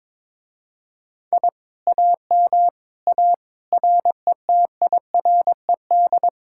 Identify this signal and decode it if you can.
Morse code, heard as a keyed tone, about 22 wpm, 700 Hz. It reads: I AM A RETIRED
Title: Speed-Racing Common QSO Phrases - 22wpm